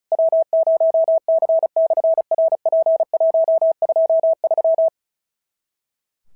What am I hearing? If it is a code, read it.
Morse code, heard as a keyed tone, about 36 wpm, 650 Hz. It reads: W0C/RP123
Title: SOTA QSO Practice 35wpm